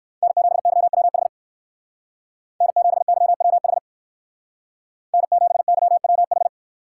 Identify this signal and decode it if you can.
Morse code, heard as a keyed tone, about 54 wpm, 700 Hz. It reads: N7YWH N7YWH N7YWH
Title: Speed Racing - USA 1x3 Call Signs 45wpm